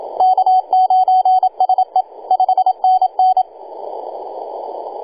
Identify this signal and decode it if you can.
Morse code, heard as a keyed tone, about 27 wpm, 750 Hz, with radio noise. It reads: K9SE 5NN
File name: e6gg-18-083-cw